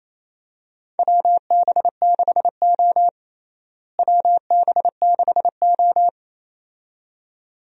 Morse code, heard as a keyed tone, about 28 wpm, 700 Hz. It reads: WB6O WB6O
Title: USA Call Signs - 4 Characters - Repeated 2x - 28wpm